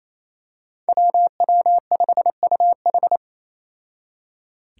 Morse code, heard as a keyed tone, about 27 wpm, 700 Hz. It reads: WW5UH
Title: Five Challenging Letter-Number Combinations 28wpm